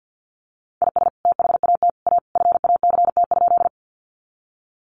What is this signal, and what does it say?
Morse code, heard as a keyed tone, about 50 wpm, 700 Hz. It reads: IS THAT A FACT?
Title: Sentences from Top 500 Words 50wpm